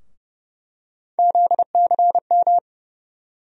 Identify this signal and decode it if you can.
Morse code, heard as a keyed tone, about 30 wpm, 700 Hz. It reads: ZCM
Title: USA Call Sign Postfixes - 3 Letters 30wpm